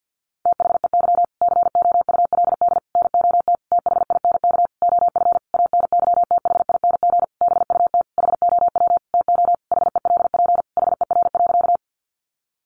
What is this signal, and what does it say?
Morse code, heard as a keyed tone, about 50 wpm, 700 Hz. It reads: THEY COULD NOT THINK OF ANYTHING BUT HOW TO HELP HER.
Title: Sentences from Top 1000 Words 50wpm